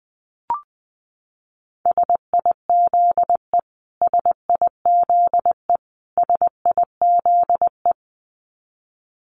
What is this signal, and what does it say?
Morse code, heard as a keyed tone, about 20 wpm, 700 Hz. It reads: SIZE SIZE SIZE
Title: Top 500 Words Repeated 20wpm